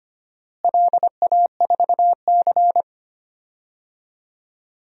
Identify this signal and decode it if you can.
Morse code, heard as a keyed tone, about 25 wpm, 700 Hz. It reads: LA4C
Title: DXCC-100 Call Signs - 4 Characters - Repeated 2x - 25wpm